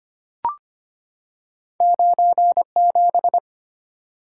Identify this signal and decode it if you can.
Morse code, heard as a keyed tone, about 25 wpm, 690 Hz. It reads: 97
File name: Two Letter-Number Combinations - 25wpm